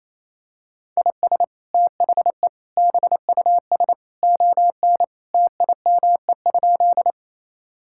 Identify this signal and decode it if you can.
Morse code, heard as a keyed tone, about 28 wpm, 700 Hz. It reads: IS THE BUS ON TIME?
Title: Sentences from Top 1000 Words - Review Part 6 - 28wpm